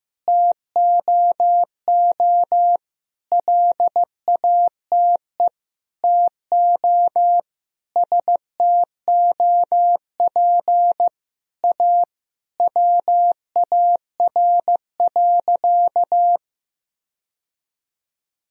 Morse code, heard as a keyed tone, about 16 wpm, 700 Hz. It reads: TOO LATE TO STOP A WAR.
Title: Sentences from Top 1000 Words - Review Part 2 - 15wpm